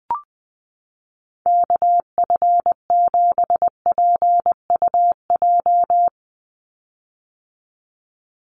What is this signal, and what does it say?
Morse code, heard as a keyed tone, about 20 wpm, 700 Hz. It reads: KF7PUJ
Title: USA Call Signs 20wpm